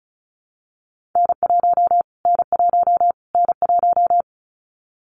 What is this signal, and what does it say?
Morse code, heard as a keyed tone, about 34 wpm, 700 Hz. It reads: N1 N1 N1
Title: Callsign Prefixes for Top 25 DXCC - Repeated 3x - 35wpm